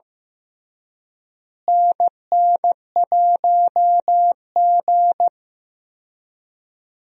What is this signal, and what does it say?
Morse code, heard as a keyed tone, about 15 wpm, 700 Hz. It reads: NN1G
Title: USA Call Signs - 4 Characters - Repeated 3x - 15wpm